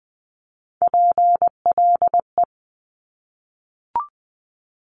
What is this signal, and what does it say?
Morse code, heard as a keyed tone, about 20 wpm, 700 Hz. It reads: PLE
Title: Top Three Letter Combinations in Words 20wpm